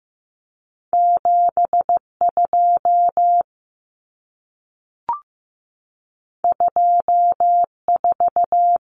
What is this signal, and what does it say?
Morse code, heard as a keyed tone, about 15 wpm, 700 Hz. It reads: 72 24
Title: Common Cut Numbers - 2-Digit 15wpm